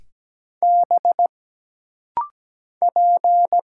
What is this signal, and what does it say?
Morse code, heard as a keyed tone, about 17 wpm, 700 Hz. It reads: B P
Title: Single Letter-Number - Rapid-Fire 17wpm